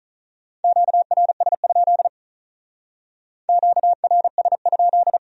Morse code, heard as a keyed tone, about 44 wpm, 700 Hz. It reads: QRS? QRS?